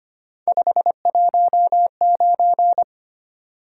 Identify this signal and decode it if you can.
Morse code, heard as a keyed tone, about 25 wpm, 700 Hz. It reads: 519
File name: Three Letter-Number Combinations - 25wpm